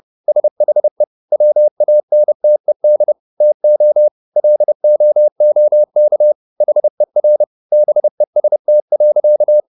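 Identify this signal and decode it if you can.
Morse code, heard as a keyed tone, about 30 wpm, 600 Hz. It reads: SHE WANTED TO LOOK HER BEST.
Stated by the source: Morse code at 30 wpm, the novel Never by Ken Follett in English, from CW med Gnister.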